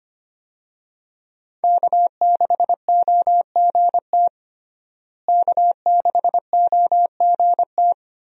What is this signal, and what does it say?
Morse code, heard as a keyed tone, about 25 wpm, 700 Hz. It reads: K6OGT K6OGT
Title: USA Call Signs - 5 Characters - Repeated 3x - 25wpm